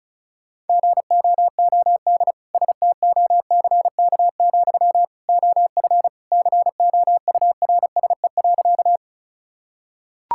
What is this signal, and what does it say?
Morse code, heard as a keyed tone, about 34 wpm, 700 Hz. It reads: GOOD STOCK, OF COURSE.
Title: Sentences from Top 1000 Words - Review Part 5 - 35wpm